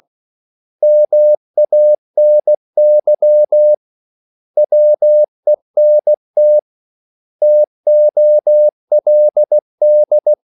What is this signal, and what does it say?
Morse code, heard as a keyed tone, about 16 wpm, 600 Hz. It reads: MANY WENT TOLD
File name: CW / Practice / 500MostCommonEnglishWords16